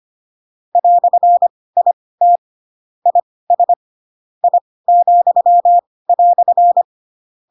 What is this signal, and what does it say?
Morse code, heard as a keyed tone, about 25 wpm, spaced as for 18 wpm, 700 Hz. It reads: "IT IS I,"
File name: Pinocchio / Pinocchio ch6 12 25 ik1hge